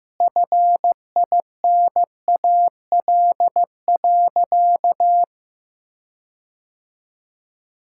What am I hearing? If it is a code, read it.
Morse code, heard as a keyed tone, about 15 wpm, 700 Hz. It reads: FINAL.
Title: Sentences from Top 1600 Words 15wpm